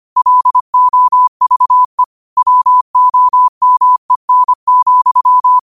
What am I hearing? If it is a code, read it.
Morse code, heard as a keyed tone, about 25 wpm, 1000 Hz. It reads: LOVE WOMEN,
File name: qotdfiles / 25wpm